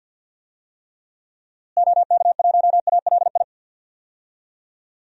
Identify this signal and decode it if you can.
Morse code, heard as a keyed tone, about 52 wpm, 700 Hz. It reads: OK1ALI
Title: DXCC-100 Call Signs - 6 Characters - Repeated 3x - 50wpm